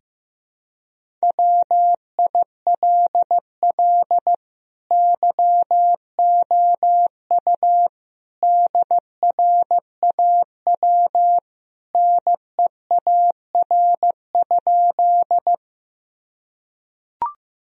Morse code, heard as a keyed tone, about 16 wpm, 700 Hz. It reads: WILL YOU DRAW NEAR?
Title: Sentences from Top 1000 Words - Review Part 1 - 15wpm